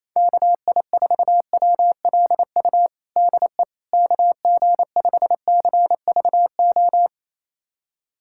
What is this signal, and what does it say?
Morse code, heard as a keyed tone, about 27 wpm, 700 Hz. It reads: KI4WLU DE KG5CVO
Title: USA Callsign DE Callsign - 6 Characters - 28wpm